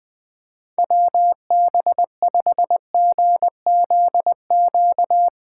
Morse code, heard as a keyed tone, about 20 wpm, 700 Hz. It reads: WB5GZQ
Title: USA Call Signs - 6 Characters - 8x Character Spacing - 20wpm